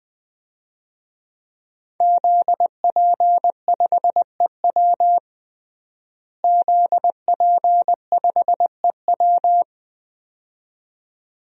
Morse code, heard as a keyed tone, about 20 wpm, 700 Hz. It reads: ZP5EW ZP5EW
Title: DXCC-100 Call Signs - 5 Characters - Repeated 2x - 20wpm